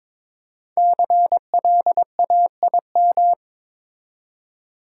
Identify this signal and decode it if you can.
Morse code, heard as a keyed tone, about 23 wpm, 700 Hz. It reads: CLAIM
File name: 5 Letter Words - First 3 Letters Given - 22wpm